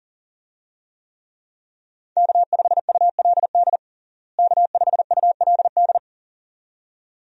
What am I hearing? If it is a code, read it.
Morse code, heard as a keyed tone, about 40 wpm, 700 Hz. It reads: K5ULD K5ULD
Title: USA Call Signs - 5 Characters - Repeated 2x - 40wpm